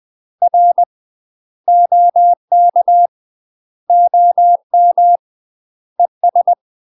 Morse code, heard as a keyed tone, about 20 wpm, 700 Hz. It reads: R OK OM ES
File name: UK MASTER QSOs TRAINING / 20WPM / UK MASTER QSOs 0047